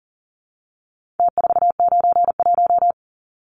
Morse code, heard as a keyed tone, about 40 wpm, 700 Hz. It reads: T491
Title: Numbers - Contest Style - 4-Digit 40wpm